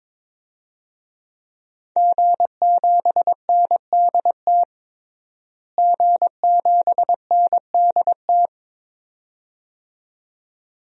Morse code, heard as a keyed tone, about 21 wpm, 700 Hz. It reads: G7NDT G7NDT
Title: DXCC-100 Call Signs - 5 Characters - Repeated 2x - 22wpm